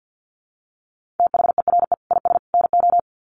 Morse code, heard as a keyed tone, about 49 wpm, 700 Hz. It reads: THERE IS NO
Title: Sets of 3 Words 50wpm